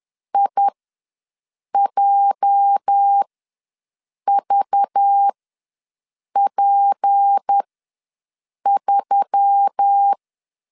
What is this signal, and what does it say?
Morse code, heard as a keyed tone, about 10 wpm, spaced as for 6 wpm, 800 Hz. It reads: IJVP3
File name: LEZIONI / 9 28 ws 10wpm 4124